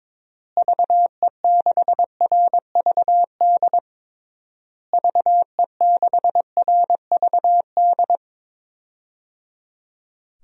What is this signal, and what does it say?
Morse code, heard as a keyed tone, about 22 wpm, 700 Hz. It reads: VE6RVD VE6RVD
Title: DXCC-100 Call Signs - 6 Characters - Repeated 3x - 22wpm